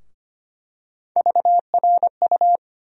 Morse code, heard as a keyed tone, about 25 wpm, 700 Hz. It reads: VRU
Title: USA Call Sign Postfixes - 3 Letters 25wpm